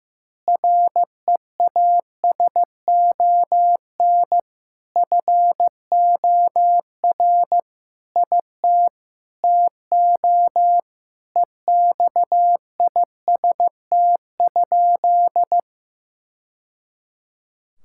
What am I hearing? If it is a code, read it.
Morse code, heard as a keyed tone, about 14 wpm, 700 Hz. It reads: REASON FOR IT TO EXIST?
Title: Sentences from Top 1000 Words - Review Part 5 - 15wpm